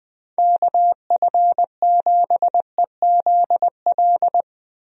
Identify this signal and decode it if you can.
Morse code, heard as a keyed tone, about 21 wpm, 700 Hz. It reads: KF7EZL
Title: USA Call Signs - 6 Characters - Repeated 2x - 20wpm